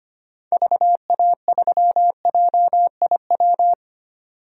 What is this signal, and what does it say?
Morse code, heard as a keyed tone, about 25 wpm, 700 Hz. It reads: VA3JIW
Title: DXCC-100 Call Signs - 6 Characters - 25wpm